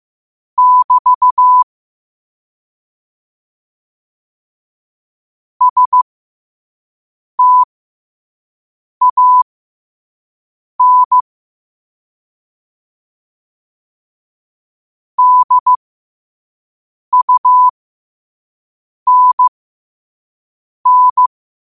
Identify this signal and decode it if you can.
Morse code, heard as a keyed tone, about 15 wpm, spaced as for 5 wpm, 1000 Hz. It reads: <BT> STAN DUNN
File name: qotdfiles / 5wpm